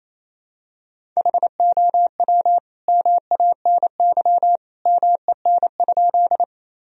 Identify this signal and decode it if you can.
Morse code, heard as a keyed tone, about 28 wpm, 700 Hz. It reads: HOW MANY MEN?
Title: Sentences from Top 1000 Words - Review Part 1 - 28wpm